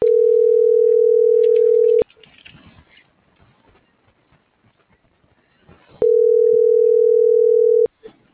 Ringback tone (North America), 440 + 480 Hz.